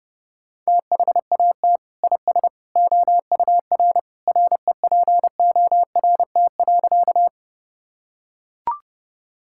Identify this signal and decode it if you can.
Morse code, heard as a keyed tone, about 30 wpm, 700 Hz. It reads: THAT IS OUR REPORT.
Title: Sentences from Top 1000 Words - Review Part 6 - 30wpm